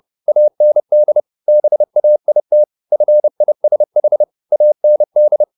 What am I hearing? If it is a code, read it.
Morse code, heard as a keyed tone, about 30 wpm, 600 Hz. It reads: AND BAIT FISH AND
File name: the old man and the sea / the old man and the sea 1 30wpm